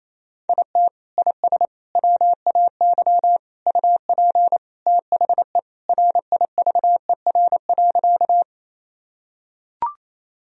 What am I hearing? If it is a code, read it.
Morse code, heard as a keyed tone, about 28 wpm, 700 Hz. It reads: IT IS WAY UP THE RIVER.
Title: Sentences from Top 1000 Words - Review Part 1 - 28wpm